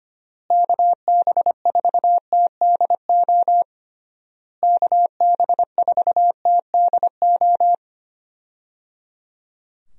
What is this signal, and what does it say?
Morse code, heard as a keyed tone, about 25 wpm, 700 Hz. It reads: KB4TDO KB4TDO
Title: USA Call Signs - 6 Characters - Repeated 3x - 25wpm